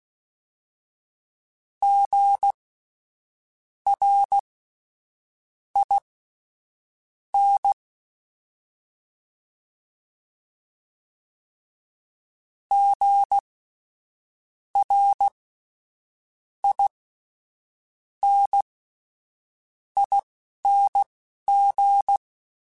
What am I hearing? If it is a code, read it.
Morse code, heard as a keyed tone, about 16 wpm, spaced as for 6 wpm, 800 Hz. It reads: GRIN GRINING
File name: Spaced Alphas / Spaced Alphas mp3s / Gs sp @10